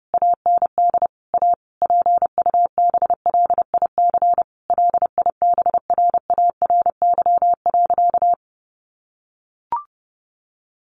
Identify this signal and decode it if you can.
Morse code, heard as a keyed tone, about 30 wpm, 700 Hz. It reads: AND A PUBLIC LIBRARY.
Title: Sentences from Top 1300 Words 30wpm